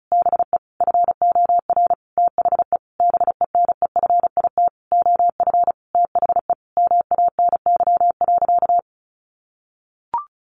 Morse code, heard as a keyed tone, about 34 wpm, 700 Hz. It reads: BE FOR THE BENEFIT OF THE MANY.
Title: Sentences from Top 1300 Words 35wpm